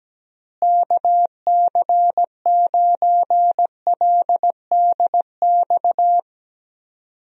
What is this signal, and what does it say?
Morse code, heard as a keyed tone, about 17 wpm, 700 Hz. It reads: KC9LDX
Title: USA Call Signs - 6 Characters - Repeated 3x - 17wpm